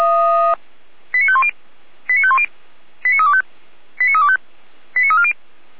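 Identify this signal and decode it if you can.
Five-tone selective call (ZVEI), heard as quick runs of five tones, tones of about 70 ms. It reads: 89510 89510 89325 89325 89340
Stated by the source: Five-tone ZVEI selective call 89230 received by SDR, from the audio server of Feuerwehr Orlamünde.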